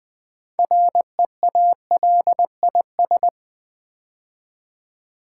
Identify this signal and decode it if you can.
Morse code, heard as a keyed tone, about 20 wpm, 700 Hz. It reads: REALIS
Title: Build A Word - 3 Letters At A Time - 9-Letter Words - 20wpm